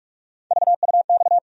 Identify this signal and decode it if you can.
Morse code, heard as a keyed tone, about 46 wpm, 700 Hz. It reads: VUX
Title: USA Call Sign Postfixes - 3 Letters 45wpm